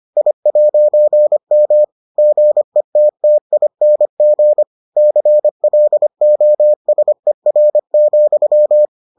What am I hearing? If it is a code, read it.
Morse code, heard as a keyed tone, about 25 wpm, 600 Hz. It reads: I'M GETTING CLOSER,